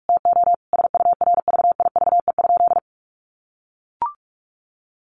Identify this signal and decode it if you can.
Morse code, heard as a keyed tone, about 44 wpm, 700 Hz. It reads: TO SURVIVE?
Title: A Sound Of Thunder 45wpm